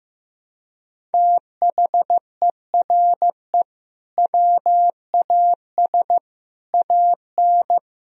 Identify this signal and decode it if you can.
Morse code, heard as a keyed tone, about 15 wpm, 700 Hz. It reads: THERE WAS AN
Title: Sentences from Top 1000 Words - Review Part 4 - 15wpm